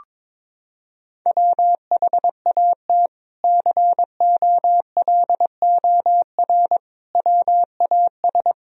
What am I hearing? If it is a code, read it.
Morse code, heard as a keyed tone, about 22 wpm, 700 Hz. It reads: WHAT COLOR WAS
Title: Sentences from Top 1000 Words - Review Part 2 - 22wpm